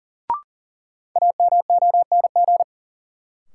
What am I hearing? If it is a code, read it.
Morse code, heard as a keyed tone, about 40 wpm, 700 Hz. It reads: AMONG
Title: Sentences from Top 500 Words - New Word Drill - 40wpm